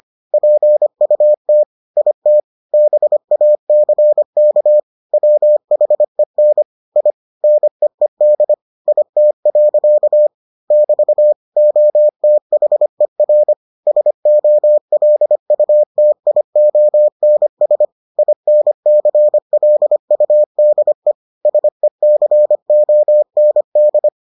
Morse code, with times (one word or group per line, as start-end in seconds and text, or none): 0.3-1.6 PUT
2.0-2.4 IT
2.7-4.8 BACK
5.1-6.6 WHEN
7.0-7.1 I
7.4-8.5 NEED
8.9-10.3 IT.
10.7-11.3 <BT>
11.6-13.5 OTHER
13.9-17.9 SOLUTIONS
18.2-21.1 INCLUDE
21.5-24.1 SECOND